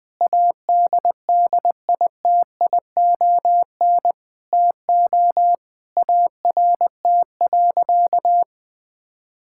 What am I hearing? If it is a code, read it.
Morse code, heard as a keyed tone, about 20 wpm, 700 Hz. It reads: ADDITION TO ART.
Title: Sentences from Top 1000 Words - Review Part 6 - 20wpm